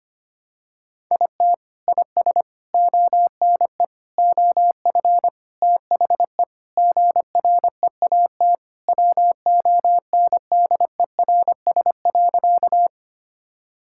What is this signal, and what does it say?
Morse code, heard as a keyed tone, about 25 wpm, 700 Hz. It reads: IT IS ONE OF THE GREAT WONDERS.
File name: practice / 25wpm / Sentences with 4-Word Phrases - 25wpm